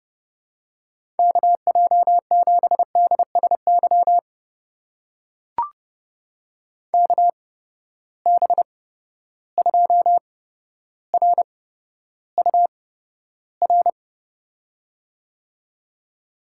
Morse code, heard as a keyed tone, about 30 wpm, spaced as for 13 wpm, 700 Hz. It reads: KJ7DSY KB2RUR
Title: USA Call Signs - 6 Characters - 8x Character Spacing - 30wpm